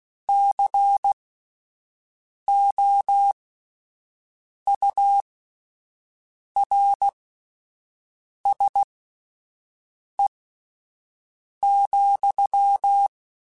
Morse code, heard as a keyed tone, about 16 wpm, spaced as for 5 wpm, 800 Hz. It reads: COURSE,
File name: (spaced) / QSOs Spaced-mp3s / WA7QOX 1 @10 sp